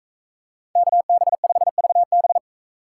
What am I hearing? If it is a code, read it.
Morse code, heard as a keyed tone, about 41 wpm, 700 Hz. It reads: KB5VB